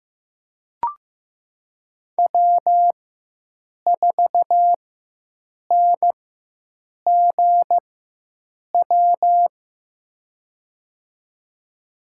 Morse code, heard as a keyed tone, about 15 wpm, spaced as for 7 wpm, 700 Hz. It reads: W4NGW
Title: USA Call Signs - 4x Character Spacing - 15wpm